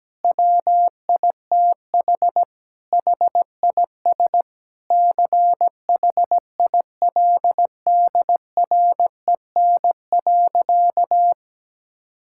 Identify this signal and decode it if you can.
Morse code, heard as a keyed tone, about 17 wpm, 700 Hz. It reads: WITH HIS CHILDREN.
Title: Sentences from Top 1600 Words 17wpm